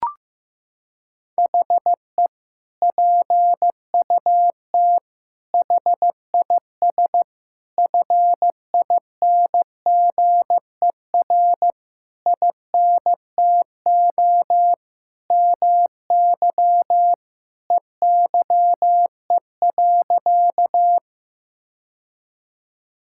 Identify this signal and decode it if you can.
Morse code, heard as a keyed tone, about 15 wpm, 700 Hz. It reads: HE PUT HIS FINGER INTO MY EYE.
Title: Sentences from Top 1600 Words 15wpm